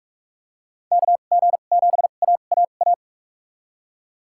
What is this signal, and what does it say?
Morse code, heard as a keyed tone, about 41 wpm, spaced as for 31 wpm, 700 Hz. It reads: KG7AAA
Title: USA Call Signs - 2x Character Spacing - 45wpm